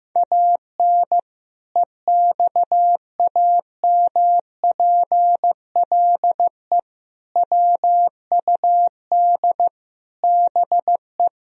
Morse code, heard as a keyed tone, about 15 wpm, 700 Hz. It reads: AN EXAMPLE WUD BE
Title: Sentences from Top 400 Words and Abbreviations 15wpm